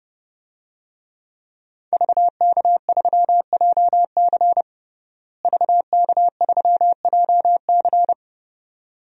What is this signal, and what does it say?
Morse code, heard as a keyed tone, about 30 wpm, 700 Hz. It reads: VK3JC VK3JC
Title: DXCC-100 Call Signs - 5 Characters - Repeated 2x - 30wpm